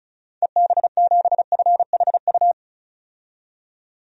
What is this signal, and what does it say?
Morse code, heard as a keyed tone, about 34 wpm, 700 Hz. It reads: EB7FHU
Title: DXCC-100 Call Signs - 6 Characters - 35wpm